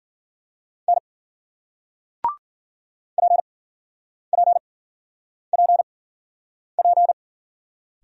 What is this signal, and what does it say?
Morse code, heard as a keyed tone, about 53 wpm, spaced as for 9 wpm, 700 Hz. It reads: N PPPP